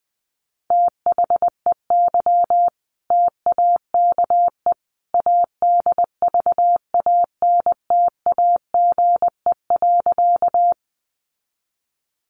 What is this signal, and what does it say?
Morse code, heard as a keyed tone, about 20 wpm, 700 Hz. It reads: THEY TAKE ADVANTAGE.
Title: Sentences from Top 900 Words 20wpm